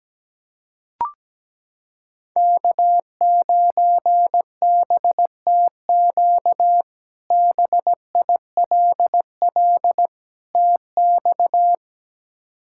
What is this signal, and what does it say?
Morse code, heard as a keyed tone, about 17 wpm, 700 Hz. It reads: K9BTQ BILL TX
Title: K1USN SST QSO Practice - 17wpm